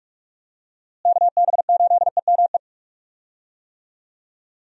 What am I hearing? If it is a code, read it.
Morse code, heard as a keyed tone, about 44 wpm, 700 Hz. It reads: KB8EME
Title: USA Call Signs - 6 Characters - Repeated 3x - 45wpm